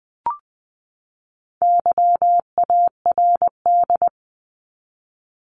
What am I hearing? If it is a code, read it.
Morse code, heard as a keyed tone, about 20 wpm, 700 Hz. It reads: YARD
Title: Top 1751-2000 Words 20wpm